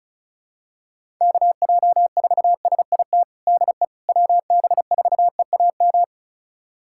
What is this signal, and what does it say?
Morse code, heard as a keyed tone, about 35 wpm, 700 Hz. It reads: KJ4SIT DE WB4EAM
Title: USA Callsign DE Callsign - 6 Characters - 35wpm